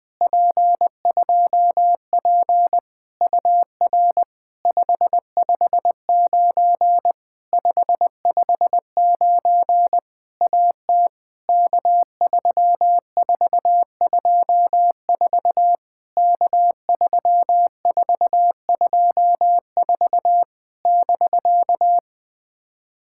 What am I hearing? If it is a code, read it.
Morse code, heard as a keyed tone, about 21 wpm, 700 Hz. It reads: P2P UR 559 559 AT K3424 K3424 <BK>
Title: POTA QSO Practice - 20wpm